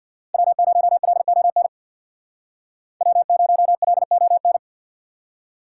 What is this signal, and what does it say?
Morse code, heard as a keyed tone, about 63 wpm, 700 Hz. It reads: W0LON W0LON